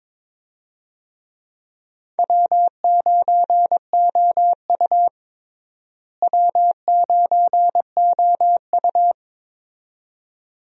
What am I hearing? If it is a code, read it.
Morse code, heard as a keyed tone, about 22 wpm, 700 Hz. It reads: W9OU W9OU
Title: USA Call Signs - 4 Characters - Repeated 2x - 22wpm